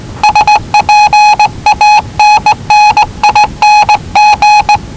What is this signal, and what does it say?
Morse code, heard as a keyed tone, about 19 wpm, 840 Hz. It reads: SPANNING
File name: WellingtonRadioZlwCloseDownOnHfAndMf / 01-Zlw 4277khz 417.5 qrt until 0925z